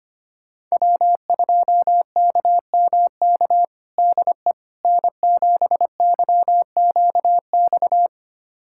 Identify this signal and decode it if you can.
Morse code, heard as a keyed tone, about 25 wpm, 700 Hz. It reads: W2KMK DE N7YQX
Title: USA Callsign DE Callsign - 5 Characters - 25wpm